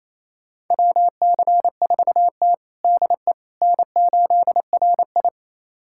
Morse code, heard as a keyed tone, about 28 wpm, 700 Hz. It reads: WC4T DE N8RI
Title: USA Callsign DE Callsign - 4 Characters - 28wpm